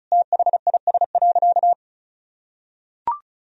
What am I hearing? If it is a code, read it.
Morse code, heard as a keyed tone, about 36 wpm, 700 Hz. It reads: THIS.
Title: Sentences from Top 1000 Words - Review Part 4 - 35wpm